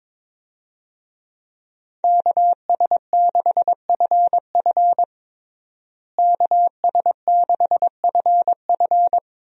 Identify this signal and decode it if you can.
Morse code, heard as a keyed tone, about 21 wpm, 700 Hz. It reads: KS6FF KS6FF
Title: USA Call Signs - 5 Characters - Repeated 2x - 22wpm